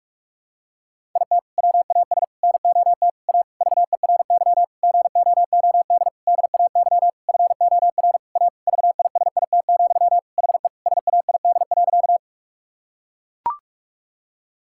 Morse code, heard as a keyed tone, about 41 wpm, 700 Hz. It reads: IT WAS NOT A VERY GOOD DAY FOR A VISIT, HE SAID.
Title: Sentences from Top 1000 Words - Review Part 6 - 45wpm